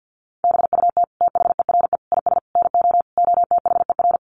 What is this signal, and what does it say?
Morse code, heard as a keyed tone, about 50 wpm, 700 Hz. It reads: BUT THERE IS NO OTHER
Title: Sentences from Top 1100 Words 50wpm